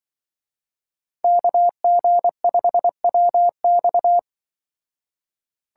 Morse code, heard as a keyed tone, about 24 wpm, 700 Hz. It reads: KG5WX
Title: Speed Racing - USA 2x2 Call Signs 20wpm